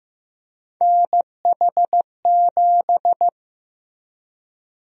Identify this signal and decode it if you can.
Morse code, heard as a keyed tone, about 15 wpm, 700 Hz. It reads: NH7
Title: USA Call Sign Prefixes - Repeated 2x - 15wpm